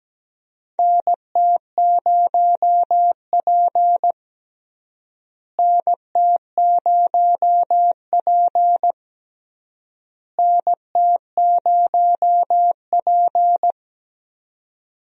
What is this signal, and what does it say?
Morse code, heard as a keyed tone, about 17 wpm, 700 Hz. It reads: NT0P NT0P NT0P
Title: USA Call Signs - 4 Characters - Repeated 3x - 17wpm